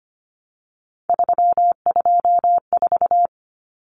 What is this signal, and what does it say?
Morse code, heard as a keyed tone, about 25 wpm, 700 Hz. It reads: 324
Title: Common Cut Numbers - 3-Digit 25wpm